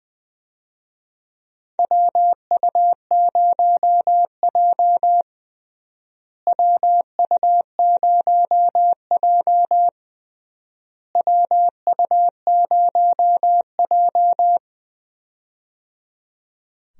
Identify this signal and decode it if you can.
Morse code, heard as a keyed tone, about 21 wpm, 700 Hz. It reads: WU0J WU0J WU0J
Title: USA Call Signs - 4 Characters - Repeated 3x - 20wpm